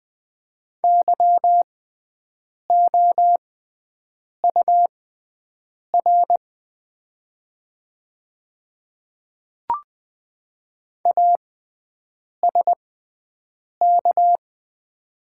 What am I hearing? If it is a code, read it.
Morse code, heard as a keyed tone, about 20 wpm, spaced as for 7 wpm, 700 Hz. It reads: YOUR ASK
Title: 6x Character Spacing - Top 200 Words - 20wpm